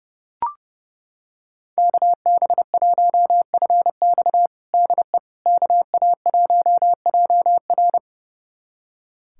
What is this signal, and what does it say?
Morse code, heard as a keyed tone, about 30 wpm, 700 Hz. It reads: KB1FX DE KA1JR
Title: USA Callsign DE Callsign - 5 Characters - 30wpm